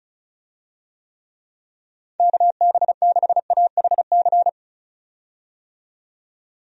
Morse code, heard as a keyed tone, about 36 wpm, 700 Hz. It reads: KB6AHC
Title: USA Call Signs - 6 Characters - 35wpm